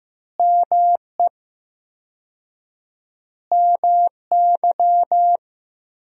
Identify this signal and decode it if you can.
Morse code, heard as a keyed tone, about 15 wpm, 700 Hz. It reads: ME MY
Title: 4x Word-Spacing - Sentences from Top 200 Words - 15wpm